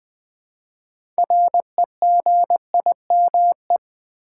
Morse code, heard as a keyed tone, about 20 wpm, 700 Hz. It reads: REGIME
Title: Build A Word - 3 Letters At A Time - 6-Letter Words - 20wpm